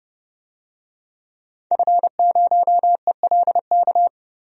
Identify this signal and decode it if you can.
Morse code, heard as a keyed tone, about 32 wpm, 700 Hz. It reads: F0ELK
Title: DXCC-100 Call Signs - 5 Characters - Repeated 2x - 30wpm